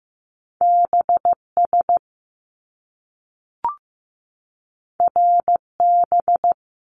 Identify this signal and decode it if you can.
Morse code, heard as a keyed tone, about 15 wpm, 700 Hz. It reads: BS RB
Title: Top Two Letter Combinations in Words 15wpm